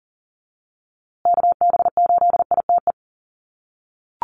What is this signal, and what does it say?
Morse code, heard as a keyed tone, about 40 wpm, 700 Hz. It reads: KB8ITE
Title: USA Call Signs Repeated 2x 40wpm